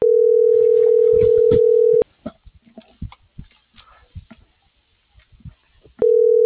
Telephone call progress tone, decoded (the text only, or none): ringback tone